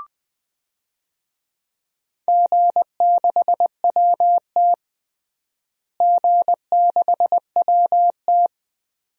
Morse code, heard as a keyed tone, about 21 wpm, 700 Hz. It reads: G6WT G6WT
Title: DXCC-100 Call Signs - 4 Characters - Repeated 2x - 20wpm